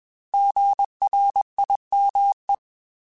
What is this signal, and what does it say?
Morse code, heard as a keyed tone, about 21 wpm, 800 Hz. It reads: GRIME
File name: Alphabetical / Alphabetical mp3s / Gs @20